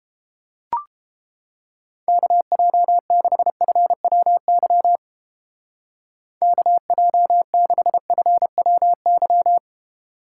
Morse code, heard as a keyed tone, about 31 wpm, 700 Hz. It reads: KJ6FWY KJ6FWY